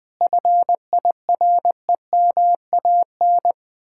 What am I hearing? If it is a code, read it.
Morse code, heard as a keyed tone, about 20 wpm, 700 Hz. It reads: FIREMAN
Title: Speed-Racing Common QSO Phrases Encore - 20wpm